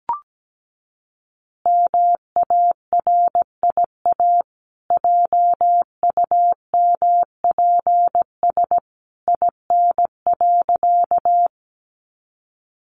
Morse code, heard as a keyed tone, about 17 wpm, 700 Hz. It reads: MARIA JUMPS IN.
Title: Easy Phrases 17wpm